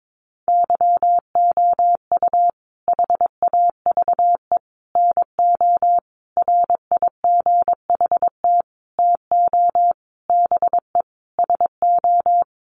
Morse code, heard as a keyed tone, about 22 wpm, 700 Hz. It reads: YOU HAVE NO RIGHT TO BE SO
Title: Sentences from Top 1100 Words 22wpm